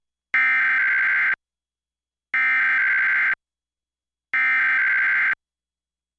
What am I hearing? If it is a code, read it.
Emergency Alert System (SAME) tones (header bursts), heard as harsh buzzing bursts. It is heard from 0.3 to 5.1 s.